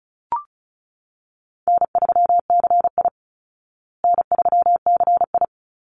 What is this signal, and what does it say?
Morse code, heard as a keyed tone, about 34 wpm, 700 Hz. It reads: N3CI N3CI
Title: Call Signs Repeated 3x - 35wpm